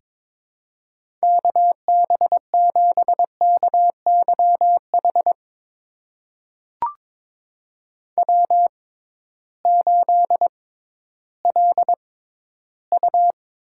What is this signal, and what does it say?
Morse code, heard as a keyed tone, about 22 wpm, spaced as for 13 wpm, 700 Hz. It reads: KB7KYH W8LU
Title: USA Call Signs - 6x Character Spacing - 22wpm